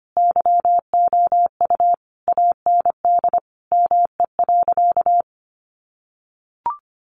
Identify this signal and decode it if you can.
Morse code, heard as a keyed tone, about 25 wpm, 700 Hz. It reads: YOU AND ME.
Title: Sentences from Top 500 Words 25wpm